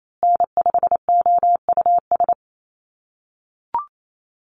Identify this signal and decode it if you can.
Morse code, heard as a keyed tone, about 28 wpm, 700 Hz. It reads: N5OUS
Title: USA Call Signs Repeated 2x 28wpm